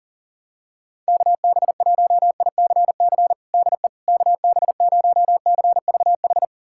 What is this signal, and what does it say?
Morse code, heard as a keyed tone, about 40 wpm, 700 Hz. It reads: KB1ICC DE KB0CVH
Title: USA Callsign DE Callsign 40wpm